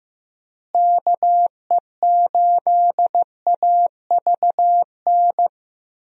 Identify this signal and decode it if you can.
Morse code, heard as a keyed tone, about 15 wpm, 700 Hz. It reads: KE8AVN
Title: USA Call Signs - 6 Characters - Repeated 3x - 15wpm